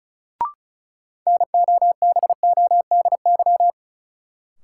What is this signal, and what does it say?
Morse code, heard as a keyed tone, about 36 wpm, 700 Hz. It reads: NOBODY
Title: Sentences from Top 1600 Words - New Word Drill - 35wpm